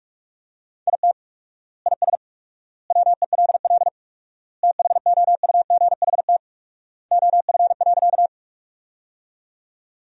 Morse code, heard as a keyed tone, about 42 wpm, 700 Hz. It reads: IT IS WELL THOUGHT OF.
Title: 4x Word-Spacing - Sentences from Top 300 Words - 45wpm